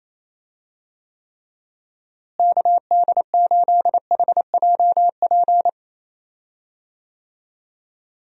Morse code, heard as a keyed tone, about 28 wpm, 700 Hz. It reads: KD8HJP
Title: USA Call Signs - 6 Characters - Repeated 2x - 28wpm